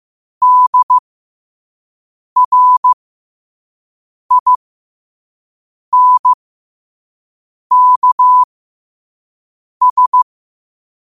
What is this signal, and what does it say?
Morse code, heard as a keyed tone, about 15 wpm, spaced as for 5 wpm, 1000 Hz. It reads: DRINKS
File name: qotdfiles / 5wpm